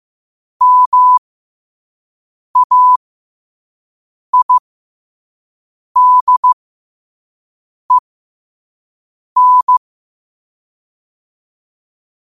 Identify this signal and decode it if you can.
Morse code, heard as a keyed tone, about 15 wpm, spaced as for 5 wpm, 1000 Hz. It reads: MAIDEN
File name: qotdfiles / 5wpm